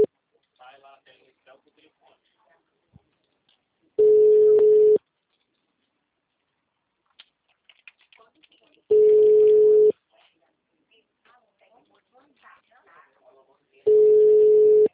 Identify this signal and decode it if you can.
Ringback tone (Europe, 425 Hz), with speech.